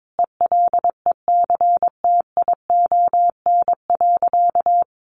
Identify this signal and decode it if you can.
Morse code, heard as a keyed tone, about 22 wpm, 700 Hz. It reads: ELECTION.
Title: Sentences from Top 1500 Words 22wpm